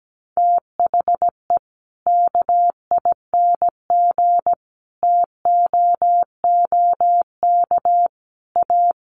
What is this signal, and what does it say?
Morse code, heard as a keyed tone, about 17 wpm, 700 Hz. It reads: THE KING TOOK A
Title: Sentences from Top 1500 Words 17wpm